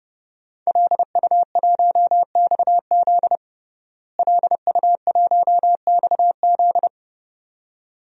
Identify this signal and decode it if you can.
Morse code, heard as a keyed tone, about 32 wpm, 700 Hz. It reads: LU1XZ LU1XZ
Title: DXCC-100 Call Signs - 5 Characters - Repeated 2x - 30wpm